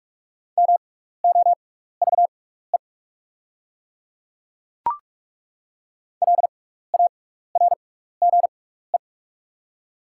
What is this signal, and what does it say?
Morse code, heard as a keyed tone, about 47 wpm, spaced as for 16 wpm, 700 Hz. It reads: MOVE LARGE
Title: 6x Character Spacing - Top 300 Words - 45wpm